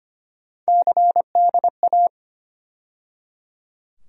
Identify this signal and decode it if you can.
Morse code, heard as a keyed tone, about 25 wpm, 700 Hz. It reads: CDA
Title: USA Call Sign Postfixes - 3 Letters 25wpm